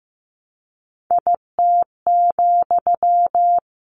Morse code, heard as a keyed tone, about 15 wpm, 700 Hz. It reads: IT,
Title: A Sound Of Thunder 15wpm